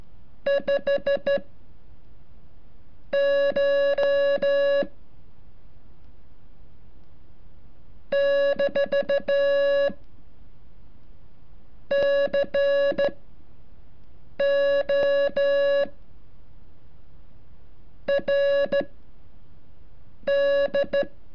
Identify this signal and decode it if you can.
Morse code, heard as a keyed tone, about 13 wpm, spaced as for 5 wpm, 560 Hz. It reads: 5 CH - CO R D
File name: curso / ejercicio 33